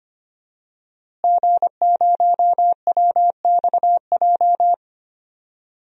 Morse code, heard as a keyed tone, about 25 wpm, 700 Hz. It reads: G0WXJ
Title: DXCC-100 Call Signs - 5 Characters - Repeated 2x - 25wpm